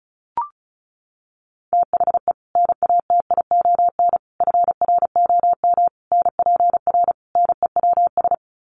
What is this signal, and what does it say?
Morse code, heard as a keyed tone, about 34 wpm, 700 Hz. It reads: THE NATION FROM NPR NEWS
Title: Sets of 5 Words 35wpm